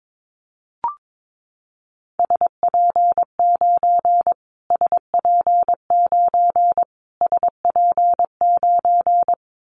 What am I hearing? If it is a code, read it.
Morse code, heard as a keyed tone, about 22 wpm, 700 Hz. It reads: SP9 SP9 SP9
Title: Callsign Prefixes for Top 25 DXCC - Repeated 3x - 22wpm